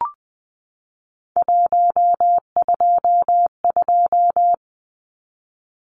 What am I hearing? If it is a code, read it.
Morse code, heard as a keyed tone, about 20 wpm, 700 Hz. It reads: 122
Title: Numbers - 3-Digit 20wpm